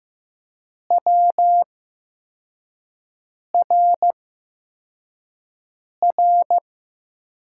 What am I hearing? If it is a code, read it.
Morse code, heard as a keyed tone, about 15 wpm, spaced as for 4 wpm, 700 Hz. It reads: WRR
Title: USA Call Signs - 8x Character Spacing - 15wpm